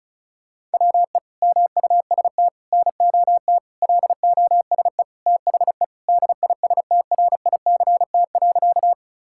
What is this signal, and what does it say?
Morse code, heard as a keyed tone, about 36 wpm, 700 Hz. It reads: WE MUST NOT LOSE THE DISTRICT.
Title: Sentences from Top 1600 Words 35wpm